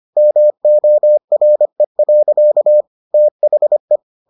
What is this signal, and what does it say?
Morse code, heard as a keyed tone, about 25 wpm, 600 Hz. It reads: MORE. THE